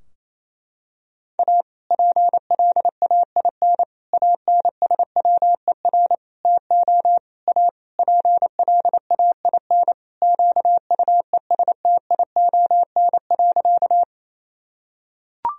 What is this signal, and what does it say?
Morse code, heard as a keyed tone, about 28 wpm, 700 Hz. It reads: A PLAIN ANSWER TO A PLAIN QUESTION.
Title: Sentences from Top 1000 Words - Review Part 2 - 28wpm